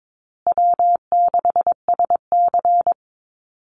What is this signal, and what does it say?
Morse code, heard as a keyed tone, about 22 wpm, 700 Hz. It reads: W6SC
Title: Easy Call Signs Repeated 3x - 22wpm